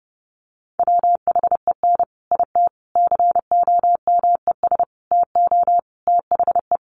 Morse code, heard as a keyed tone, about 30 wpm, 700 Hz. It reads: WHEN IT COMES TO THE
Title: Sets of 5 Words 30wpm